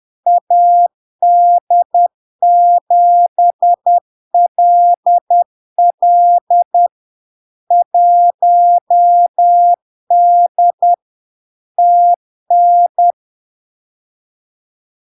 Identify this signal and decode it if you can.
Morse code, heard as a keyed tone, about 10 wpm, 700 Hz. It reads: AD7LL 1D TN
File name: LARK-FD-2022-Log-in-CW-10wpm-700-Hz